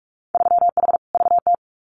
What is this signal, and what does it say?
Morse code, heard as a keyed tone, about 46 wpm, 700 Hz. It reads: 3H VT
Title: Winter Field Day Exchanges Encore 45wpm